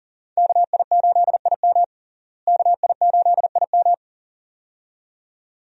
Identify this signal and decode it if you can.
Morse code, heard as a keyed tone, about 40 wpm, 700 Hz. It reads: KI8IM KI8IM
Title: USA Call Signs - 5 Characters - Repeated 2x - 40wpm